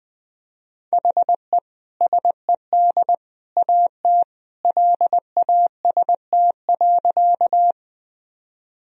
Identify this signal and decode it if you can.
Morse code, heard as a keyed tone, about 20 wpm, 700 Hz. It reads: HE SED AT LAST.
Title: Sentences from Top 300 Words and Abbreviations 20wpm